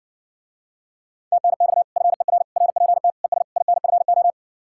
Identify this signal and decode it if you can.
Morse code, heard as a keyed tone, about 70 wpm, 700 Hz. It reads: TNX FER RPT ES INFO